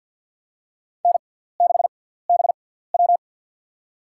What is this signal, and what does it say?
Morse code, heard as a keyed tone, about 52 wpm, spaced as for 17 wpm, 700 Hz. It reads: N6BW
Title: USA Call Signs - 6x Character Spacing - 50wpm